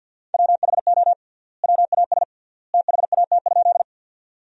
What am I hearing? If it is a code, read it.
Morse code, heard as a keyed tone, about 50 wpm, 700 Hz. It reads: WHO WAS THAT?
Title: 3x Word-Spacing - Sentences from Top 100 Words - 50wpm